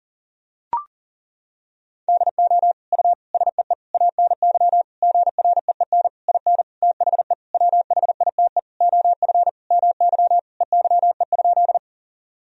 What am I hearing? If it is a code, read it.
Morse code, heard as a keyed tone, about 40 wpm, 700 Hz. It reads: DO U SEE ANY GREEN IN THE WHITE OF MY EYE?
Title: Sentences from Top 500 Words and Abbreviations 40wpm